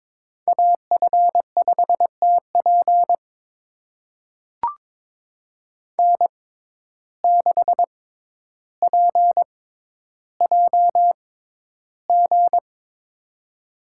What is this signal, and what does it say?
Morse code, heard as a keyed tone, about 22 wpm, spaced as for 11 wpm, 700 Hz. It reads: AF5TP N6PJG
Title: USA Call Signs - 5 Characters - 6x Character Spacing - 22wpm